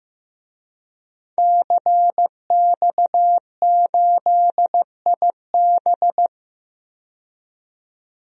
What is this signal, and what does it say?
Morse code, heard as a keyed tone, about 15 wpm, 700 Hz. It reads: CX8IB
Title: DXCC-100 Call Signs - 5 Characters - 15wpm